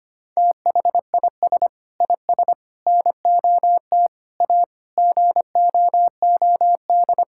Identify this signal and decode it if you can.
Morse code, heard as a keyed tone, about 25 wpm, 700 Hz. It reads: THIS IS NOT A GOOD
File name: practice / 25wpm / Sentences with 4-Word Phrases - 25wpm